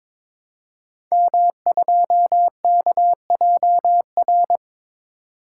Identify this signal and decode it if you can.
Morse code, heard as a keyed tone, about 22 wpm, 690 Hz. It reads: M2KJR
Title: Code Groups - 5 Letter:Numbers - 22wpm